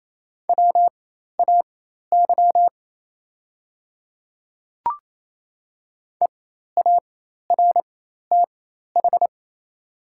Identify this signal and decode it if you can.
Morse code, heard as a keyed tone, about 27 wpm, spaced as for 13 wpm, 690 Hz. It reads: WAY EARTH
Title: 4x Character Spacing - Top 200 Words - 28wpm